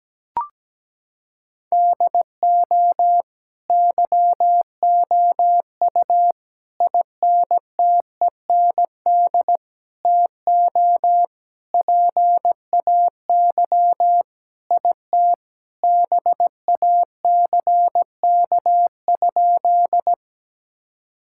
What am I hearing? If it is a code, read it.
Morse code, heard as a keyed tone, about 17 wpm, 700 Hz. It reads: DO YOU INTEND TO PAY IT BACK?
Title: Sentences from Top 1600 Words 17wpm